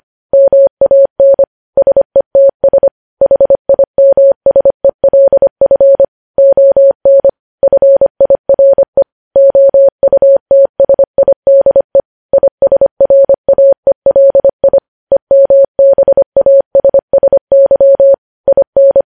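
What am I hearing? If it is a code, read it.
Morse code, heard as a keyed tone, about 25 wpm, 550 Hz. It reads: MAN SETS HIMSELF ON FIRE OUTSIDE ISRAELI EMBASSY IN
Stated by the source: Morse code news headlines sent at 25 wpm.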